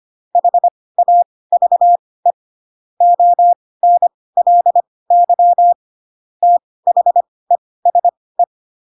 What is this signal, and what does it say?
Morse code, heard as a keyed tone, about 25 wpm, spaced as for 18 wpm, 700 Hz. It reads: HAVE ONLY THESE
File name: Pinocchio / Pinocchio ch7 12 25 ik1hge